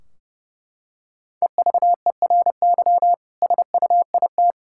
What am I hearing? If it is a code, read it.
Morse code, heard as a keyed tone, about 32 wpm, 700 Hz. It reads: EVERY SUIT
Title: Sentences from Top 1600 Words 30wpm